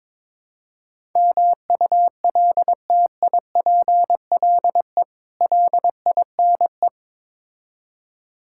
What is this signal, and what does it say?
Morse code, heard as a keyed tone, about 22 wpm, 700 Hz. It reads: MULTIPLE LINE
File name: practice / 22wpm / Build A Phrase - 5-Word Phrases - 22wpm